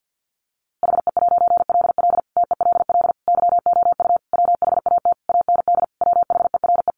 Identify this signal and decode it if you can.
Morse code, heard as a keyed tone, about 50 wpm, 700 Hz. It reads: HE'LL TELL YOU WHAT AND WHERE
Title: A Sound Of Thunder 50wpm